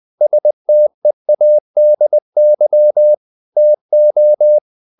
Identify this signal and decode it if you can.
Morse code, heard as a keyed tone, about 20 wpm, 600 Hz. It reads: STEADY TO
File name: the old man and the sea part 2 20wpm